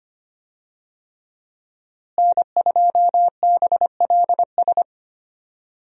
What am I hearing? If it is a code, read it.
Morse code, heard as a keyed tone, about 25 wpm, 700 Hz. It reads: N2BLS
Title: USA Call Signs - 5 Characters - Repeated 3x - 25wpm